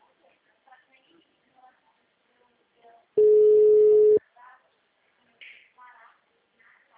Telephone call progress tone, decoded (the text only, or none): ringback tone